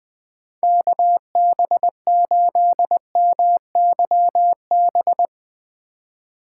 Morse code, heard as a keyed tone, about 20 wpm, 700 Hz. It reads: KB8MYB
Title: Speed Racing - USA Call Signs 17wpm